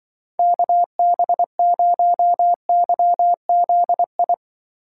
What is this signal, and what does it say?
Morse code, heard as a keyed tone, about 25 wpm, 700 Hz. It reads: KB0YZI
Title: Speed Racing - USA 2x3 Call Signs 20wpm